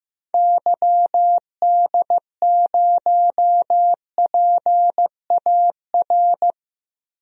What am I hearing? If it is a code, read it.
Morse code, heard as a keyed tone, about 15 wpm, 700 Hz. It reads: YD0PAR
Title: DXCC-100 Call Signs - 6 Characters - Repeated 2x - 15wpm